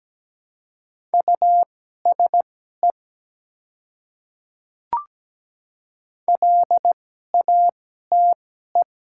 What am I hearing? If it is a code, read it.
Morse code, heard as a keyed tone, about 17 wpm, spaced as for 12 wpm, 690 Hz. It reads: USE LATE
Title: 2x Character Spacing - Top 300 Words - 17wpm